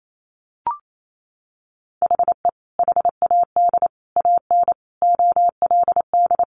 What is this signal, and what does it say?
Morse code, heard as a keyed tone, about 28 wpm, 700 Hz. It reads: HE HAD AN OLD
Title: Sentences from Top 1100 Words 28wpm